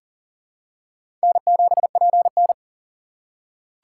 Morse code, heard as a keyed tone, about 40 wpm, 700 Hz. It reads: N7PN
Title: USA Call Signs - 4 Characters - 3x Character Spacing - 40wpm